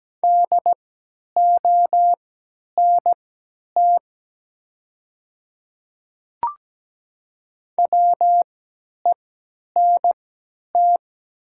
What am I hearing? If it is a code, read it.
Morse code, heard as a keyed tone, about 17 wpm, spaced as for 10 wpm, 700 Hz. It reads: DONT WENT
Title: 3x Character Spacing - Top 300 Words - 17wpm